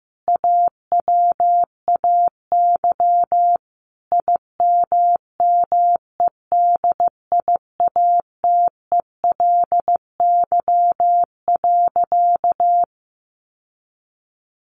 Morse code, heard as a keyed tone, about 15 wpm, 700 Hz. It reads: AWAY IMMEDIATELY.
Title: Sentences from Top 1100 Words 15wpm